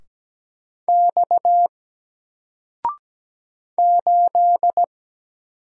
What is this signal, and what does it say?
Morse code, heard as a keyed tone, about 17 wpm, 700 Hz. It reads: X 8
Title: Single Letter-Number 17wpm